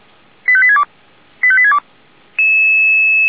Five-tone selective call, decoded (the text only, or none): 85862 85862